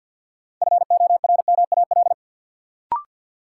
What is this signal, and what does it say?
Morse code, heard as a keyed tone, about 53 wpm, 700 Hz. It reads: FORMAL
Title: Sentences from Top 1300 Words - New Word Drill Alt - 50wpm